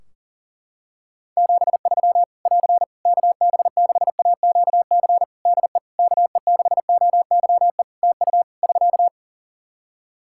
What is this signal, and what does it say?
Morse code, heard as a keyed tone, about 40 wpm, 700 Hz. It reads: 73 <AR> KB6AQC DE KE6OYE TU <SK>
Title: Practice QSO - Exchange 4 - Ending - 40wpm